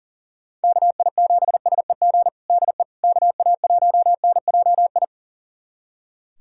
Morse code, heard as a keyed tone, about 40 wpm, 700 Hz. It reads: KI7SEG DE KA1NJI
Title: USA Callsign DE Callsign - 6 Characters - 40wpm